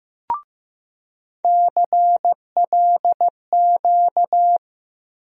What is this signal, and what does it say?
Morse code, heard as a keyed tone, about 15 wpm, 700 Hz. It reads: CLQ
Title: USA Call Sign Postfixes - 3 Letters 15wpm